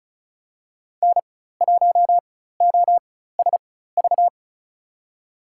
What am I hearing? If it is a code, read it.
Morse code, heard as a keyed tone, about 34 wpm, spaced as for 16 wpm, 700 Hz. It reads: N1OSV
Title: USA Call Signs - 5 Characters - 4x Character Spacing - 35wpm